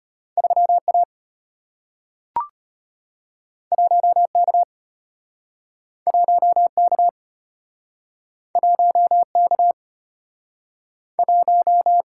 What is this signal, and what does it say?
Morse code, heard as a keyed tone, about 34 wpm, 700 Hz. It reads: 3 A 1 K 1 K 1 K 1